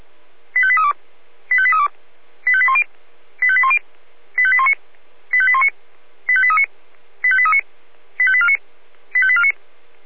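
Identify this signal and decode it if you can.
Five-tone selective call (ZVEI), heard as quick runs of five tones, tones of about 80 ms. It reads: 85832 85832 85810 85810 85819 85819 85839 85839 85849 85849
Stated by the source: ZVEI five-tone call 85830 received by SDR.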